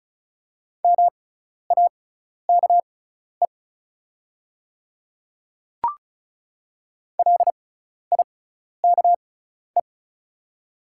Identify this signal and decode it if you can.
Morse code, heard as a keyed tone, about 36 wpm, spaced as for 12 wpm, 690 Hz. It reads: MAKE LIKE
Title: 6x Character Spacing - Top 200 Words - 35wpm